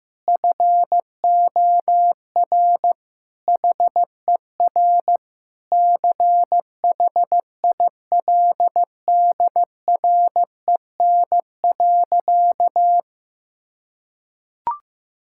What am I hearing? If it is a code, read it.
Morse code, heard as a keyed tone, about 15 wpm, 700 Hz. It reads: FOR HER CHILDREN.
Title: Sentences from Top 1000 Words - Review Part 6 - 15wpm